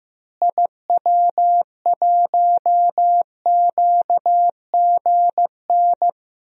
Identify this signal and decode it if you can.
Morse code, heard as a keyed tone, about 15 wpm, 700 Hz. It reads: IW1QGN
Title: DXCC-100 Call Signs - 6 Characters - Repeated 2x - 15wpm